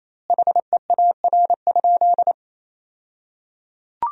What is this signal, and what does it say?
Morse code, heard as a keyed tone, about 28 wpm, 700 Hz. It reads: HEAR?
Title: 2x Word-Spacing - Sentences from Top 300 Words - 28wpm